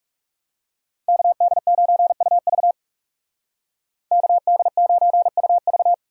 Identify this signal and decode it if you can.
Morse code, heard as a keyed tone, about 39 wpm, 700 Hz. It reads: KD9UV KD9UV